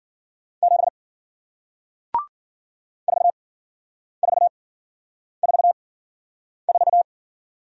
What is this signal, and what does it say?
Morse code, heard as a keyed tone, about 55 wpm, spaced as for 9 wpm, 700 Hz. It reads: 7 4444